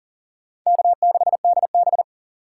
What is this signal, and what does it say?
Morse code, heard as a keyed tone, about 40 wpm, 700 Hz. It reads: K6DB
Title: USA Call Signs - 4 Characters - 2x Character Spacing - 40wpm